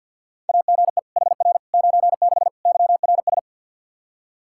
Morse code, heard as a keyed tone, about 50 wpm, 700 Hz. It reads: AGE HR 96 YRS
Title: K1USN SST QSO Practice 50wpm